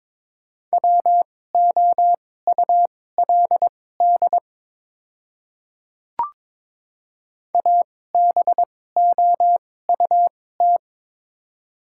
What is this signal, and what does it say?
Morse code, heard as a keyed tone, about 22 wpm, spaced as for 16 wpm, 700 Hz. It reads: WOULD ABOUT
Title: 2x Character Spacing - Top 100 Words - 22wpm